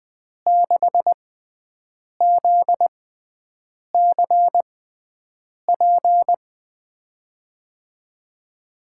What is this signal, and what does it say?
Morse code, heard as a keyed tone, about 20 wpm, spaced as for 7 wpm, 700 Hz. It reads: 6ZCP
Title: USA Call Signs - 6 Characters - 6x Character Spacing - 20wpm